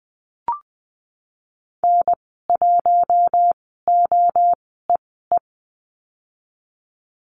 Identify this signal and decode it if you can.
Morse code, heard as a keyed tone, about 20 wpm, spaced as for 15 wpm, 700 Hz. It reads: N1OEE
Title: 042 Lesson 8g - Number 1 - Call Signs 20wpm